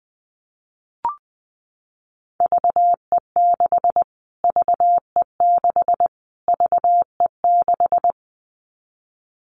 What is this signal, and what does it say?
Morse code, heard as a keyed tone, about 20 wpm, 700 Hz. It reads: VE6 VE6 VE6
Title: Callsign Prefixes for Top 25 DXCC - Repeated 3x - 20wpm